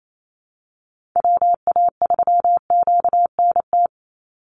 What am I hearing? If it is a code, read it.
Morse code, heard as a keyed tone, about 28 wpm, 700 Hz. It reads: WA3QNT
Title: Call Signs Repeated 2x - 28wpm